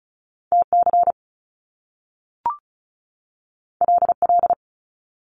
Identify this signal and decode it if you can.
Morse code, heard as a keyed tone, about 36 wpm, 700 Hz. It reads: TC LL
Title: Top Two Letter Combinations in Words 35wpm